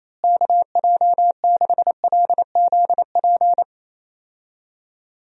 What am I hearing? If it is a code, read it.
Morse code, heard as a keyed tone, about 28 wpm, 700 Hz. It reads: KJ6LZP
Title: USA Call Signs - 6 Characters - Repeated 2x - 28wpm